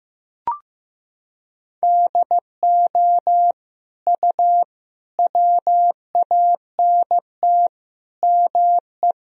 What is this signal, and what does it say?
Morse code, heard as a keyed tone, about 15 wpm, 700 Hz. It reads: DO U WANT ME
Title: Sentences from Top 200 Words and Abbreviations 15wpm